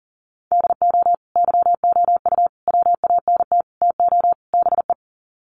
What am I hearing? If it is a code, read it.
Morse code, heard as a keyed tone, about 40 wpm, 700 Hz. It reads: DO YOU WANT TO BE
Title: Sets of 5 Words Encore2 40wpm